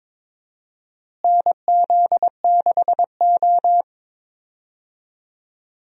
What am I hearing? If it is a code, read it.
Morse code, heard as a keyed tone, about 22 wpm, 700 Hz. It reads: NZ6O
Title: USA Call Signs - 4 Characters - Repeated 3x - 22wpm